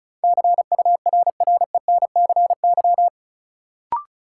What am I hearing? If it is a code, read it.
Morse code, heard as a keyed tone, about 34 wpm, 700 Hz. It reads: CURRENCY
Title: Sentences from Top 1500 Words - New Word Drill - 35wpm